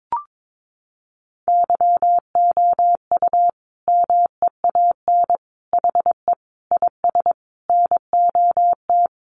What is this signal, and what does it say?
Morse code, heard as a keyed tone, about 22 wpm, 700 Hz. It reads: YOU MEAN HE IS NOT
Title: Sentences from Top 1200 Words 22wpm